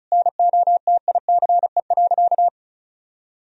Morse code, heard as a keyed tone, about 36 wpm, 700 Hz. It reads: NOTICE.
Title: Sentences from Top 1000 Words - Review Part 2 - 35wpm